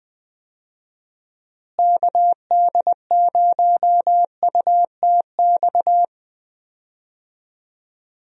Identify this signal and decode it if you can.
Morse code, heard as a keyed tone, about 20 wpm, 700 Hz. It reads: KD0UTX
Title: USA Call Signs - 6 Characters - Repeated 2x - 20wpm